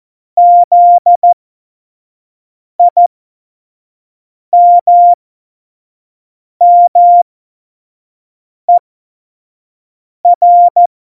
Morse code, heard as a keyed tone, about 14 wpm, spaced as for 5 wpm, 700 Hz. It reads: ZIMMER
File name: Morsekurs / Witze / Witz 25 Juli 17, Tempo 25 F70